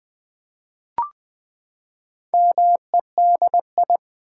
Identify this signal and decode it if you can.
Morse code, heard as a keyed tone, about 20 wpm, 700 Hz. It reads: MEDI
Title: Build A Word - 4 Letters At A Time - 8-Letter Words - 20wpm